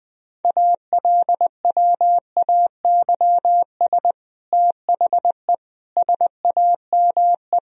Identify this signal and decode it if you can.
Morse code, heard as a keyed tone, about 19 wpm, 700 Hz. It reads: ALWAYS THE SAME
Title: Sentences from Top 1000 Words - Review Part 1 - 20wpm